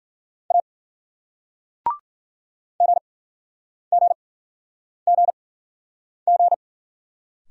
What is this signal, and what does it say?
Morse code, heard as a keyed tone, about 53 wpm, spaced as for 9 wpm, 700 Hz. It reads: A GGGG